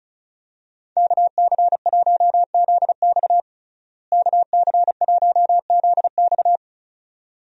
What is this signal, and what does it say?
Morse code, heard as a keyed tone, about 34 wpm, 700 Hz. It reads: KC1ZX KC1ZX
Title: USA Call Signs - 5 Characters - Repeated 2x - 35wpm